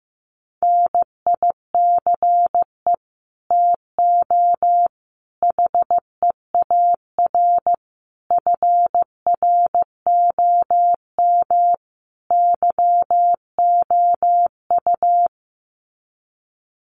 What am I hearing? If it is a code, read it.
Morse code, heard as a keyed tone, about 15 wpm, 700 Hz. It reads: NICE TO HEAR FROM YOU
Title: Common QSO Phrases 15wpm